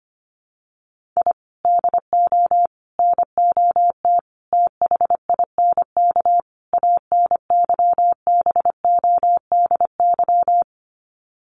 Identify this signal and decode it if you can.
Morse code, heard as a keyed tone, about 25 wpm, 700 Hz. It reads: I DO NOT THINK ANYBODY
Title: Sets of 5 Words 25wpm